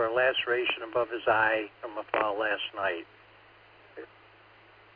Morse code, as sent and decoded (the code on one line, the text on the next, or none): none